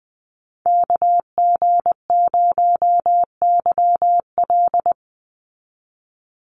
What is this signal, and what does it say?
Morse code, heard as a keyed tone, about 20 wpm, 700 Hz. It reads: KG0YL
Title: Call Signs Repeated 2x - 20wpm